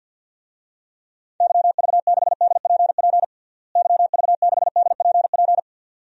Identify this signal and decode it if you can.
Morse code, heard as a keyed tone, about 52 wpm, 700 Hz. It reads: YV6DPP YV6DPP
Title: DXCC-100 Call Signs - 6 Characters - Repeated 2x - 50wpm